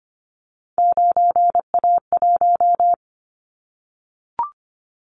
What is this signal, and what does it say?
Morse code, heard as a keyed tone, about 25 wpm, 700 Hz. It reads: 9A1
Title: Callsign Prefixes for Top 50 DXCC - 25wpm